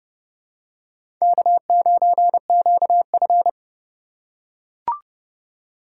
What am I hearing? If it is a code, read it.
Morse code, heard as a keyed tone, about 30 wpm, 700 Hz. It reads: K9QF
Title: USA Call Signs - 2x Character Spacing - 30wpm